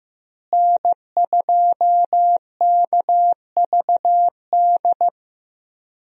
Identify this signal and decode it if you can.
Morse code, heard as a keyed tone, about 15 wpm, 700 Hz. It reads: N2KVD
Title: USA Call Signs - 5 Characters - Repeated 2x - 15wpm